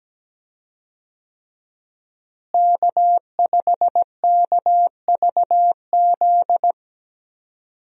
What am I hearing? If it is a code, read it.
Morse code, heard as a keyed tone, about 17 wpm, 700 Hz. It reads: K5KVZ
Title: USA Call Signs - 5 Characters - Repeated 3x - 17wpm